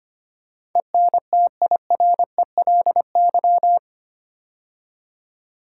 Morse code, heard as a keyed tone, about 25 wpm, 690 Hz. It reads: ENTIRELY
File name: 8 Letter Words - First 3 Letters Given - 25wpm